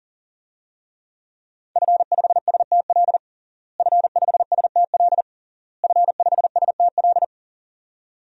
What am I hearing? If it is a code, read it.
Morse code, heard as a keyed tone, about 40 wpm, 700 Hz. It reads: F5STL F5STL F5STL
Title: DXCC-100 Call Signs - 5 Characters - Repeated 3x - 40wpm